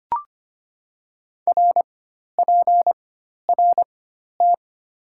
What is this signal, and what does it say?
Morse code, heard as a keyed tone, about 25 wpm, spaced as for 12 wpm, 700 Hz. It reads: RPRT
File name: Top 90 QSO Elements - 4x Character Spacing - 25wpm